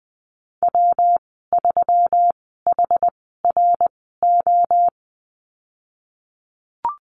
Morse code, heard as a keyed tone, about 20 wpm, spaced as for 15 wpm, 700 Hz. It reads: W3HRO
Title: 122 Lesson 20g - Number 3 - Call Signs 20wpm